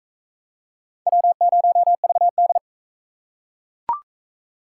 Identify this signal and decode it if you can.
Morse code, heard as a keyed tone, about 41 wpm, 700 Hz. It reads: W0VD